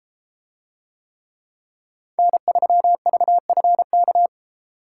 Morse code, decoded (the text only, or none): N3VFK